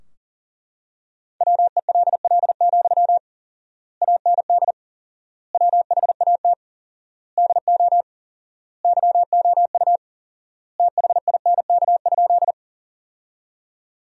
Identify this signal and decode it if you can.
Morse code, heard as a keyed tone, about 41 wpm, 700 Hz. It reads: WELL, AND WHAT DO YOU THINK?
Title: 4x Word-Spacing - Sentences from Top 200 Words - 40wpm